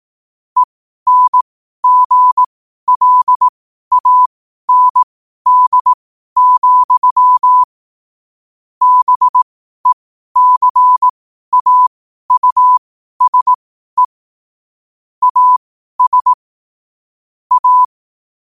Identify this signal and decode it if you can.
Morse code, heard as a keyed tone, about 18 wpm, spaced as for 12 wpm, 1000 Hz. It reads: ENGLAND, BECAUSE AS A
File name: qotdfiles / 12wpm